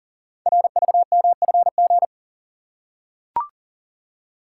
Morse code, heard as a keyed tone, about 38 wpm, 700 Hz. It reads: RVMFG
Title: Code Groups - 5 Letter:Numbers - 40wpm